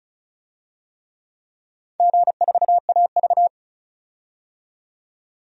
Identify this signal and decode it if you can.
Morse code, heard as a keyed tone, about 36 wpm, 700 Hz. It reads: G4AV
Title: DXCC-100 Call Signs - 4 Characters - 35wpm